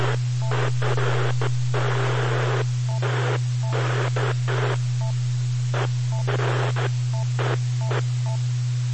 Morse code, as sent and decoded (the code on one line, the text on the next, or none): none